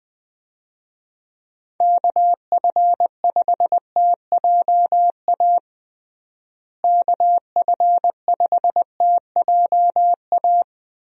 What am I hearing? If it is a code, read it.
Morse code, heard as a keyed tone, about 20 wpm, 700 Hz. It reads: KF5TJA KF5TJA
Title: USA Call Signs - 6 Characters - Repeated 2x - 20wpm